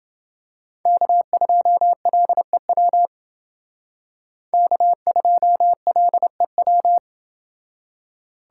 Morse code, heard as a keyed tone, about 28 wpm, 700 Hz. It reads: K2LEW K2LEW